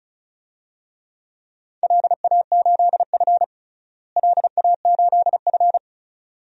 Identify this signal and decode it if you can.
Morse code, heard as a keyed tone, about 34 wpm, 700 Hz. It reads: LA8F LA8F
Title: DXCC-100 Call Signs - 4 Characters - Repeated 2x - 35wpm